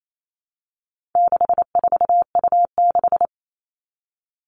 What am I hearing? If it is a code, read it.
Morse code, heard as a keyed tone, about 28 wpm, 700 Hz. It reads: 64U6
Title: All Cut Numbers - 4-Digit 28wpm